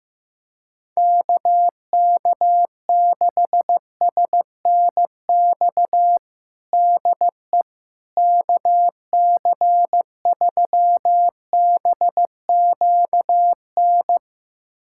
Morse code, heard as a keyed tone, about 15 wpm, 700 Hz. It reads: KK6SNX DE KC3BQN
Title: USA Callsign DE Callsign - 6 Characters - 15wpm